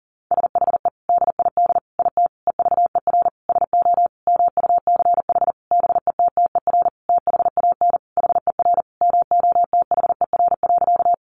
Morse code, heard as a keyed tone, about 40 wpm, 700 Hz. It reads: SHE DID IT EVER SO MUCH BETTER THAN HER MOTHER.
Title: Sentences from Top 400 Words 40wpm